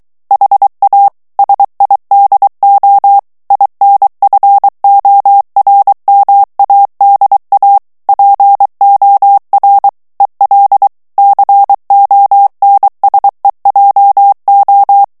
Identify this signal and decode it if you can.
Morse code, heard as a keyed tone, about 23 wpm, 800 Hz. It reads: HA SIDO INFORMADA POR EL CONSEJO
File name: morse mp3 / ESPAGNOL TEXTE EN CLAIR 2 VITESSE 22 MOTS MINUTES